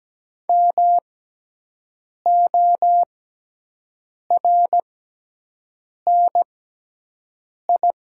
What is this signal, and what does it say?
Morse code, heard as a keyed tone, about 17 wpm, spaced as for 6 wpm, 700 Hz. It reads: MORNI
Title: 6x Character Spacing - Top 500 Words - 17wpm